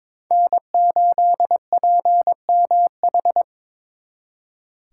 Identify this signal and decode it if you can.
Morse code, heard as a keyed tone, about 26 wpm, 700 Hz. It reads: N8PMH
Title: Speed Racing - USA 1x3 Call Signs 22wpm